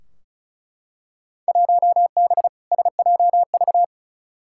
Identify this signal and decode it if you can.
Morse code, heard as a keyed tone, about 36 wpm, 700 Hz. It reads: 1B SJV
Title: Field Day Exchanges - 35wpm